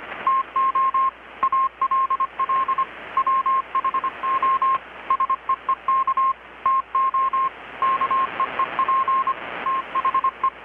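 Morse code, heard as a keyed tone, about 25 wpm, 1060 Hz, with radio noise. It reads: TO ALL WHO SEEK TO KEEP THE
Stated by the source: Morse from KPH's Night of Nights XX maritime radiotelegraphy event, received in Chiapas, Mexico about 3,800 km away on a Kenwood R-600 with a grounded 10 m long-wire antenna.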